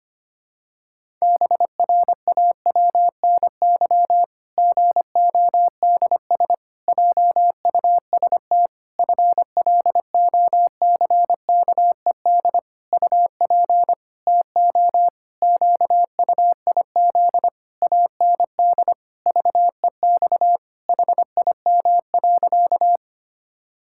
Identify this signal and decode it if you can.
Morse code, heard as a keyed tone, about 25 wpm, 700 Hz. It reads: BRAWNY GODS JUST FLOCKED UP TO QUIZ AND VEX HIM.
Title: Pangram Challenge v2 - 25wpm